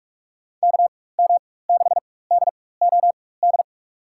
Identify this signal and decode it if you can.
Morse code, heard as a keyed tone, about 42 wpm, spaced as for 21 wpm, 700 Hz. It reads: KM6DOD
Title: USA Call Signs - 6 Characters - 4x Character Spacing - 45wpm